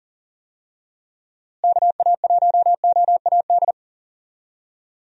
Morse code, heard as a keyed tone, about 40 wpm, 700 Hz. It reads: KA1OAD
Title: USA Call Signs - 6 Characters - Repeated 3x - 40wpm